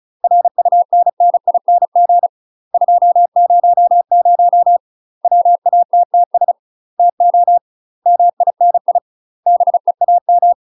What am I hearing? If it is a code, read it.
Morse code, heard as a keyed tone, about 36 wpm, 700 Hz. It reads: RUNNING 200 WATTS TO MINI BEAM
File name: UK MASTER QSOs TRAINING / 35WPM / UK MASTER QSOs 0022